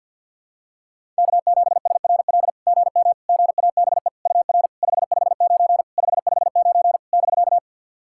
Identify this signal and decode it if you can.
Morse code, heard as a keyed tone, about 50 wpm, 690 Hz. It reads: K7SRL GM GABE UR 559 559 <BK>
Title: SOTA QSO Practice 50wpm